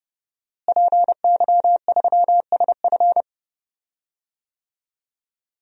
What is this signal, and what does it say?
Morse code, heard as a keyed tone, about 32 wpm, 700 Hz. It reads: PY3SF
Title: DXCC-100 Call Signs - 5 Characters - Repeated 2x - 30wpm